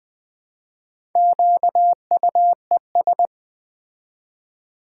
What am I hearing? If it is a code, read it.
Morse code, heard as a keyed tone, about 20 wpm, 700 Hz. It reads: QUES
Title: Build A Word - 2 Letters At A Time - 8-Letter Words - 20wpm